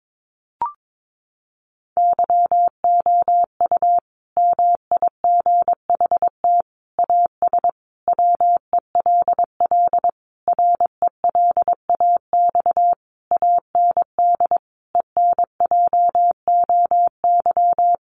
Morse code, heard as a keyed tone, about 22 wpm, 700 Hz. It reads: YOU MIGHT AS WELL RELAX AND ENJOY
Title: Sentences from Top 1400 Words 22wpm